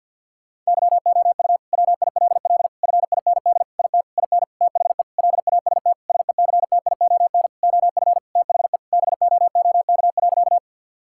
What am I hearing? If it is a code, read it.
Morse code, heard as a keyed tone, about 50 wpm, 700 Hz. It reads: YOU WILL FIND IT IN THE LAST SECTION OF THE BOOK.
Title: Sentences from Top 1000 Words - Review Part 5 - 50wpm